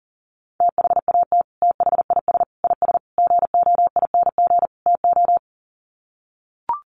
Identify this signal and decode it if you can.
Morse code, heard as a keyed tone, about 37 wpm, 700 Hz. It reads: THAT THIS IS GOING TO
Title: Sets of 5 Words Encore2 40wpm